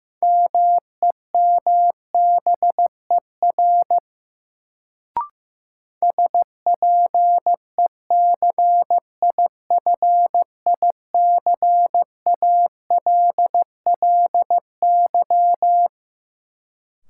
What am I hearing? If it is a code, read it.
Morse code, heard as a keyed tone, about 15 wpm, 700 Hz. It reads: MEMBER SPECIFICALLY
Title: Sentences from Top 1100 Words - New Word Drill - 15wpm